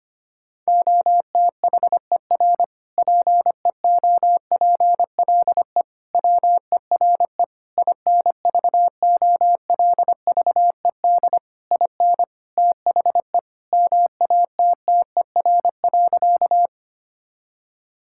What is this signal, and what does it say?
Morse code, heard as a keyed tone, about 25 wpm, 700 Hz. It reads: OTHER PEOPLE WERE INVOLVED IN THE MATTER.
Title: Sentences from Top 1000 Words - Review Part 5 - 25wpm